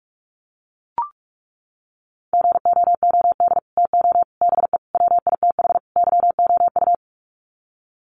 Morse code, heard as a keyed tone, about 44 wpm, 700 Hz. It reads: GOOD TO BE WITH YOU
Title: Sets of 5 Words Encore2 45wpm